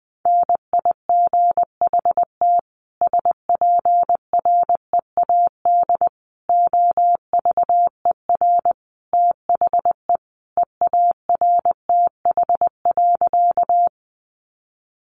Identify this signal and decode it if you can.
Morse code, heard as a keyed tone, about 20 wpm, 700 Hz. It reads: NIGHT SPREAD OVER THE EARTH.
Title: Sentences from Top 1200 Words 20wpm